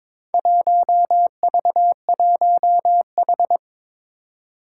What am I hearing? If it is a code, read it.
Morse code, heard as a keyed tone, about 22 wpm, 700 Hz. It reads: 1V1H
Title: Four Challenging Letter-Number Combinations 22wpm